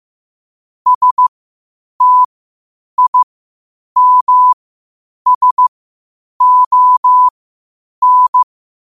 Morse code, heard as a keyed tone, about 15 wpm, spaced as for 8 wpm, 1000 Hz. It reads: STIMSON
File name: qotdfiles / 8wpm